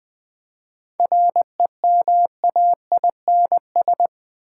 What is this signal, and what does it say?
Morse code, heard as a keyed tone, about 20 wpm, 700 Hz. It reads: REMAINS
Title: 7 Letter Words - First 3 Letters Given - 20wpm